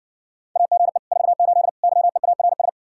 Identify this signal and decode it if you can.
Morse code, heard as a keyed tone, about 60 wpm, 700 Hz. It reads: AGE 48 YEARS